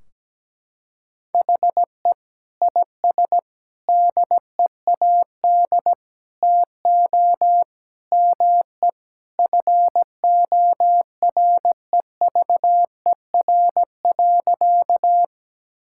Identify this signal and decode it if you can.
Morse code, heard as a keyed tone, about 17 wpm, 700 Hz. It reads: HE IS DEAD TO ME FOREVER.
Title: Sentences from Top 1600 Words 17wpm